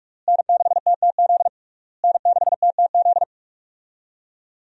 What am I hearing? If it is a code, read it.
Morse code, heard as a keyed tone, about 46 wpm, 700 Hz. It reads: N6TTZ N6TTZ
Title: USA Call Signs - 5 Characters - Repeated 3x - 45wpm